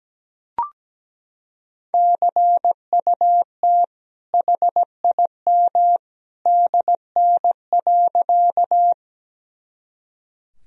Morse code, heard as a keyed tone, about 17 wpm, 700 Hz. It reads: CUT HIM DN.
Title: Sentences from Top 300 Words and Abbreviations 17wpm